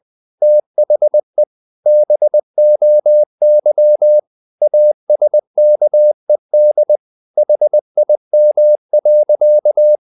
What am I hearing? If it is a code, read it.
Morse code, heard as a keyed tone, about 20 wpm, 600 Hz. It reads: THE BOY ASKED HIM.
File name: the old man and the sea / the old man and the sea part 1 20wpm